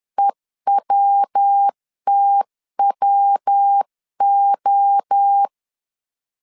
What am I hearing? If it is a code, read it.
Morse code, heard as a keyed tone, about 10 wpm, 800 Hz. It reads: EWTWO
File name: 10wpm